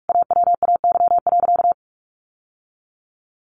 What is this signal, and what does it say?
Morse code, heard as a keyed tone, about 44 wpm, 700 Hz. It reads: AWAY.
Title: Sentences from Top 600 Words 45wpm